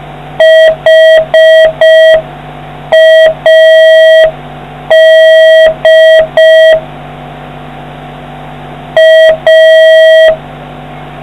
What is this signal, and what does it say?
Morse code, heard as a keyed tone, about 5 wpm, 1950 Hz, with noise. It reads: HAD A